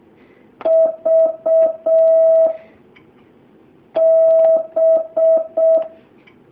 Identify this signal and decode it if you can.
Morse code, heard as a keyed tone, about 6 wpm, 650 Hz. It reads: V B
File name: jh8cbh / 2019cw